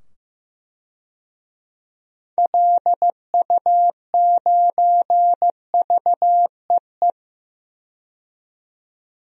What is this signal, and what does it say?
Morse code, heard as a keyed tone, about 16 wpm, 700 Hz. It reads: LU9VEE
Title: DXCC-100 Call Signs - 6 Characters - 15wpm